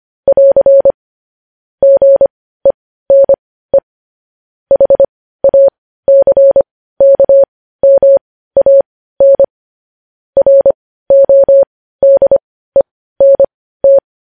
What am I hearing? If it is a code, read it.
Morse code, heard as a keyed tone, about 25 wpm, spaced as for 15 wpm, 550 Hz. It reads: <AR> GENE HACKMAN RODENT
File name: podcast / News headlines-1x-25x15